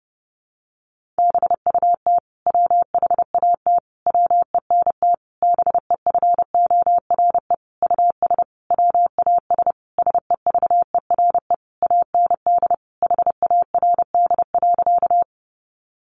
Morse code, heard as a keyed tone, about 28 wpm, 700 Hz. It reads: BUT WHAT WENT BEFORE US WAS SEVERE AND HARD.
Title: Sentences from Top 1400 Words 30wpm